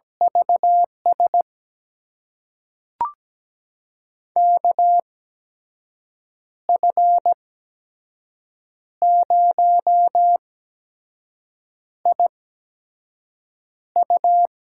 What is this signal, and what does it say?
Morse code, heard as a keyed tone, about 17 wpm, spaced as for 5 wpm, 700 Hz. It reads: VS KF0IU
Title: USA Call Signs - 5 Characters - 8x Character Spacing - 17wpm